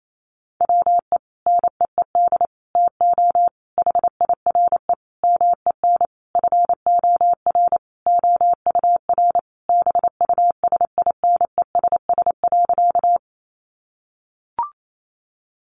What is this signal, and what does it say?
Morse code, heard as a keyed tone, about 28 wpm, 700 Hz. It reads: WE NEED TO HIRE MEN FOR OUR BUSINESS.
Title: Sentences from Top 1400 Words 28wpm